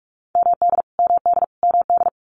Morse code, heard as a keyed tone, about 45 wpm, 700 Hz. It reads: MD MD MD
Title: US States Repeated 3x 45wpm